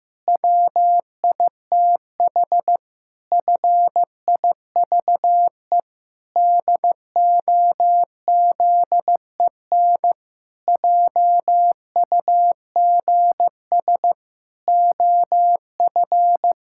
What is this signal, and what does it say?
Morse code, heard as a keyed tone, about 15 wpm, 700 Hz. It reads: WITH FIVE DOZEN JUGS OF
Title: Pangram Challenge v2 - 15wpm